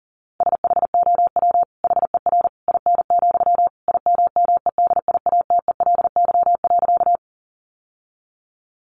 Morse code, heard as a keyed tone, about 40 wpm, 700 Hz. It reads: SHOW HER IN, IMMEDIATELY.
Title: Sentences from Top 1100 Words 40wpm